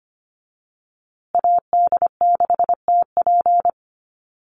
Morse code, heard as a keyed tone, about 25 wpm, 700 Hz. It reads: AD6TP
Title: USA Call Signs Repeated 3x 25wpm